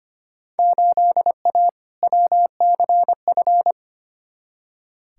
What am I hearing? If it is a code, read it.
Morse code, heard as a keyed tone, about 25 wpm, 700 Hz. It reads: 8A WCF
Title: Field Day Exchanges Encore - 25wpm